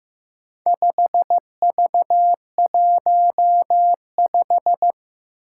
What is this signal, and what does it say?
Morse code, heard as a keyed tone, about 15 wpm, 700 Hz. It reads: 5V15
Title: Four Challenging Letter-Number Combinations 15wpm